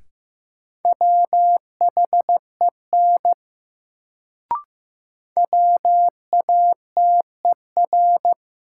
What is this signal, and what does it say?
Morse code, heard as a keyed tone, about 15 wpm, 700 Hz. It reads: WHEN WATER
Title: Sentences from Top 100 Words - New Word Drill - 15wpm